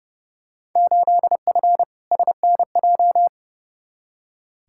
Morse code, heard as a keyed tone, about 30 wpm, 700 Hz. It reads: 8F SNJ
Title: Field Day Exchanges Encore - 30wpm